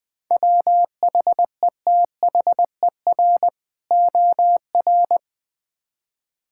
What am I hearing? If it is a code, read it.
Morse code, heard as a keyed tone, about 20 wpm, 700 Hz. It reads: WHETHER OR
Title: Build A Phrase - 3-Word Phrases - 20wpm